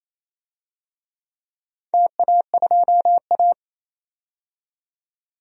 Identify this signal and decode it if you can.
Morse code, heard as a keyed tone, about 28 wpm, 700 Hz. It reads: TA2A
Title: DXCC-100 Call Signs - 4 Characters - 28wpm